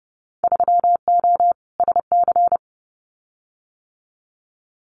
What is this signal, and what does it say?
Morse code, heard as a keyed tone, about 32 wpm, 700 Hz. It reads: 3O SC
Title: Winter Field Day Exchanges 30wpm